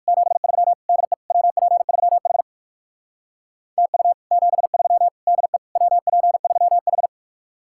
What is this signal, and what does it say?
Morse code, heard as a keyed tone, about 59 wpm, 700 Hz. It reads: 73 DE WP3H TU 73 DE WP3H